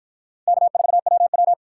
Morse code, heard as a keyed tone, about 44 wpm, 700 Hz. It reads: K4WW